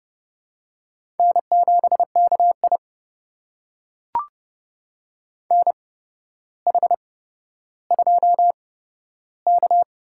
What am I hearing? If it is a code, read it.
Morse code, heard as a keyed tone, about 30 wpm, spaced as for 13 wpm, 700 Hz. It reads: N7KI NH2K
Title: USA Call Signs - 4 Characters - 8x Character Spacing - 30wpm